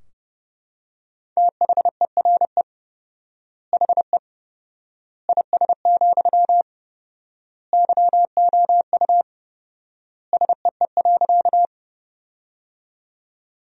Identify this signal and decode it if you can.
Morse code, heard as a keyed tone, about 30 wpm, 700 Hz. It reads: THERE HE IS, YOU SEE.
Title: 4x Word-Spacing - Sentences from Top 100 Words - 30wpm